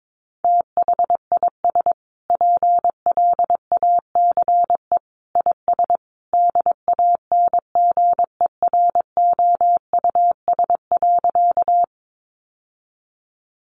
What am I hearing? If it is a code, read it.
Morse code, heard as a keyed tone, about 22 wpm, 700 Hz. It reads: THIS PLACE IS DANGEROUS.
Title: Sentences from Top 1200 Words 22wpm